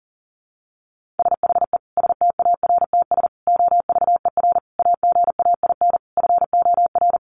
Morse code, heard as a keyed tone, about 40 wpm, 700 Hz. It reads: SHE STARTS OVER AGAIN FOR
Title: Easy Phrases 40wpm